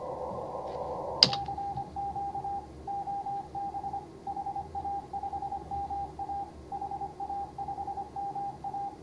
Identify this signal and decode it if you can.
Morse code, heard as a keyed tone, about 24 wpm, 810 Hz, with noise. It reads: CQ OB HA5MA HA5MA